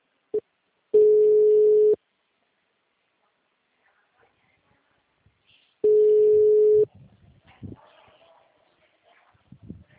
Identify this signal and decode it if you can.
Ringback tone (Europe, 425 Hz).